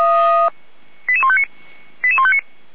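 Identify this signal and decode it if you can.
Five-tone selective call (ZVEI), heard as quick runs of five tones, tones of about 70 ms. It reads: 88159
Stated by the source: ZVEI five-tone call received by SDR.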